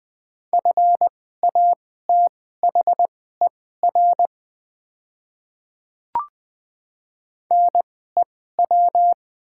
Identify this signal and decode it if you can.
Morse code, heard as a keyed tone, about 21 wpm, spaced as for 15 wpm, 700 Hz. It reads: FATHER NEW
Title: 2x Character Spacing - Top 200 Words - 20wpm